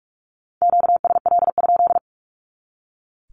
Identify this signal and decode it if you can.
Morse code, heard as a keyed tone, about 44 wpm, 700 Hz. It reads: QSL?
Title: Q-Codes 45wpm